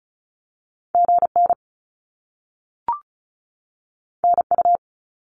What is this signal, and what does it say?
Morse code, heard as a keyed tone, about 36 wpm, 700 Hz. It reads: GN NU
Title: Top Two Letter Combinations in Words 35wpm